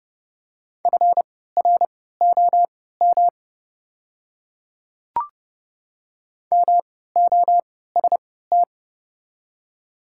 Morse code, heard as a keyed tone, about 29 wpm, spaced as for 17 wpm, 700 Hz. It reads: FROM MOST
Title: 3x Character Spacing - Top 100 Words - 30wpm